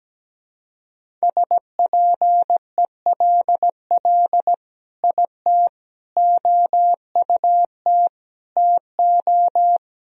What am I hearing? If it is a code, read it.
Morse code, heard as a keyed tone, about 17 wpm, 700 Hz. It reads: SPELL IT OUT TO
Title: Sentences from Top 1000 Words - Review Part 1 - 17wpm